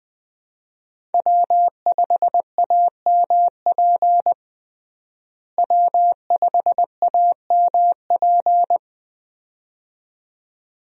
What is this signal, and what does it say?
Morse code, heard as a keyed tone, about 20 wpm, 700 Hz. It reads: W5AMP W5AMP
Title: USA Call Signs - 5 Characters - Repeated 2x - 20wpm